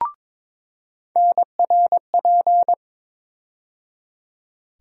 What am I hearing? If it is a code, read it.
Morse code, heard as a keyed tone, about 22 wpm, 700 Hz. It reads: NRP
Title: USA Call Sign Postfixes - 3 Letters 22wpm